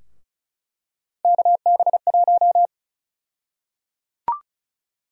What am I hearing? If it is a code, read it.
Morse code, heard as a keyed tone, about 36 wpm, 700 Hz. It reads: KB1
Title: USA Call Sign Prefixes - 35wpm